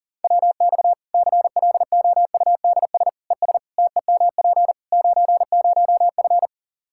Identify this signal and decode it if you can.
Morse code, heard as a keyed tone, about 40 wpm, 700 Hz. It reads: WX CLOUDS ES TEMP 90F
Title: Practice QSO - Exchange 2 - Info - 40wpm